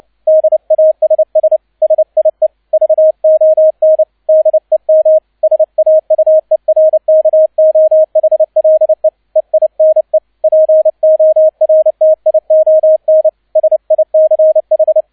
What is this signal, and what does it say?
Morse code, heard as a keyed tone, about 29 wpm, 630 Hz, with radio noise. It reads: DASS SIE VON DEM SAUERKOHLE EINE PORTION SICH
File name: 002 Zweiter Streich 30wpm